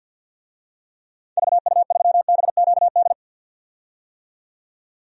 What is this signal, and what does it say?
Morse code, heard as a keyed tone, about 52 wpm, 700 Hz. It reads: VU3BQD
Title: DXCC-100 Call Signs - 6 Characters - 50wpm